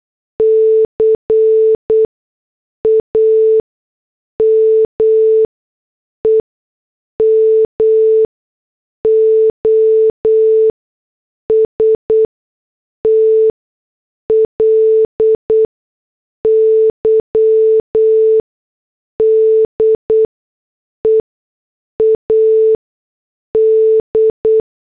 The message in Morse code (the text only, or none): CAME MOSTLY DEAD